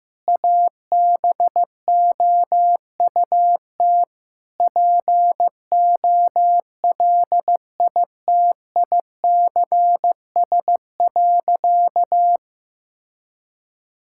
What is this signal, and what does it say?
Morse code, heard as a keyed tone, about 15 wpm, 700 Hz. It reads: ABOUT POLITICS.
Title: Sentences from Top 1000 Words - Review Part 6 - 15wpm